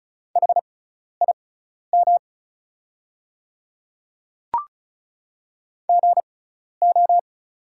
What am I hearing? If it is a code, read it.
Morse code, heard as a keyed tone, about 36 wpm, spaced as for 12 wpm, 700 Hz. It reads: HIM GO
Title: 6x Character Spacing - Top 100 Words - 35wpm